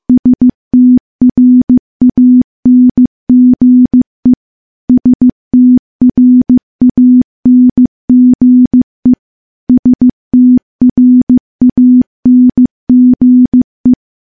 Morse code, heard as a keyed tone, about 15 wpm, 250 Hz. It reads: STRANGE STRANGE STRANGE
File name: morse-training-15WPM